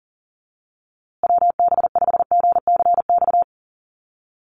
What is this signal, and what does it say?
Morse code, heard as a keyed tone, about 40 wpm, 700 Hz. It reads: WB5GCX
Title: USA Call Signs Repeated 2x 40wpm